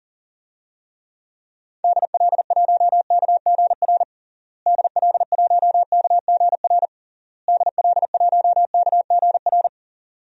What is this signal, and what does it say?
Morse code, heard as a keyed tone, about 40 wpm, 700 Hz. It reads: DL1KGR DL1KGR DL1KGR
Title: DXCC-100 Call Signs - 6 Characters - Repeated 3x - 40wpm